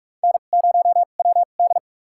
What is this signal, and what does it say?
Morse code, heard as a keyed tone, about 42 wpm, spaced as for 31 wpm, 700 Hz. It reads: N0WD
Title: USA Call Signs - 4 Characters - 2x Character Spacing - 45wpm